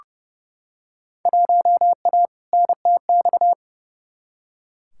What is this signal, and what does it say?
Morse code, heard as a keyed tone, about 30 wpm, 700 Hz. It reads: 1A NTX
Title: Field Day Exchanges - 30wpm